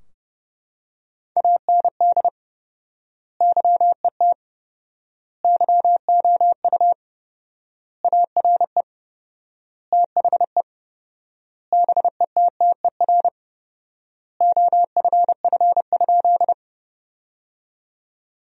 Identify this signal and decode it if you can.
Morse code, heard as a keyed tone, about 32 wpm, 700 Hz. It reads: AND YET YOU ARE THE BETTER OFF?
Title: 4x Word-Spacing - Sentences from Top 500 Words - 30wpm